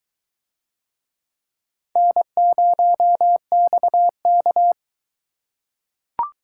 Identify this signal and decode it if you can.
Morse code, heard as a keyed tone, about 21 wpm, 700 Hz. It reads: N0XK